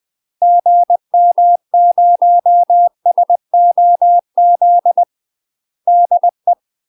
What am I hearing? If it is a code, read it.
Morse code, heard as a keyed tone, about 20 wpm, 700 Hz. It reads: GM0SOZ DE
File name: UK MASTER QSOs TRAINING / 20WPM / UK MASTER QSOs 0011